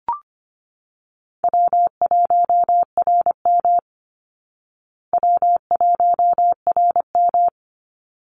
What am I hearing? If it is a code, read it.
Morse code, heard as a keyed tone, about 25 wpm, 700 Hz. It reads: W1RM W1RM
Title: Easy Call Signs Repeated 2x - 25wpm